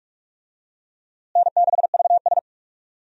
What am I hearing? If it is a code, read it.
Morse code, heard as a keyed tone, about 46 wpm, 700 Hz. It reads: N6VS
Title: USA Call Signs - 4 Characters - Repeated 3x - 45wpm